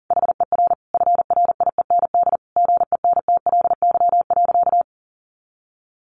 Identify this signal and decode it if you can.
Morse code, heard as a keyed tone, about 40 wpm, 700 Hz. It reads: HER FRIEND GENTLY.
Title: Sentences from Top 1500 Words 40wpm